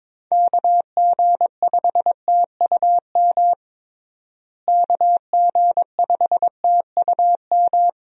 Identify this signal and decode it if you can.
Morse code, heard as a keyed tone, about 22 wpm, 700 Hz. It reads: KG5TUM KG5TUM
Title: USA Call Signs - 6 Characters - Repeated 3x - 22wpm